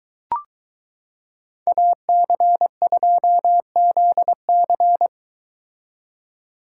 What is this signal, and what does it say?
Morse code, heard as a keyed tone, about 22 wpm, 700 Hz. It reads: AC2ZC